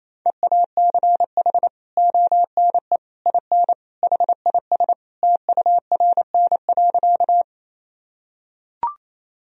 Morse code, heard as a keyed tone, about 28 wpm, 700 Hz. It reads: EACH ONE IN HIS TURN.
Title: Sentences from Top 1000 Words - Review Part 1 - 28wpm